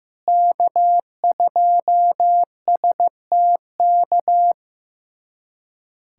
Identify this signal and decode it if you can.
Morse code, heard as a keyed tone, about 15 wpm, 700 Hz. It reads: K2STK
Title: USA Callsign DE Callsign - 5 Characters - 15wpm